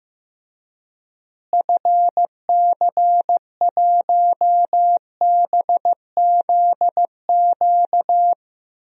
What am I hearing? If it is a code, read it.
Morse code, heard as a keyed tone, about 14 wpm, 700 Hz. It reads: FC1BZQ
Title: DXCC-100 Call Signs - 6 Characters - Repeated 2x - 15wpm